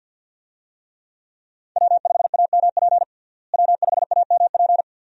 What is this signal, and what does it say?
Morse code, heard as a keyed tone, about 46 wpm, 700 Hz. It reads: W5AMP W5AMP
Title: USA Call Signs - 5 Characters - Repeated 2x - 50wpm